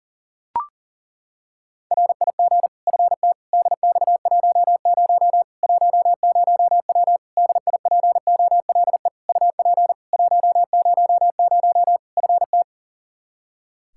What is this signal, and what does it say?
Morse code, heard as a keyed tone, about 40 wpm, 700 Hz. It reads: RIG FT DX10 10W DIPOLE UP 100 FT
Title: Practice QSO - Exchange 2 - Info - 40wpm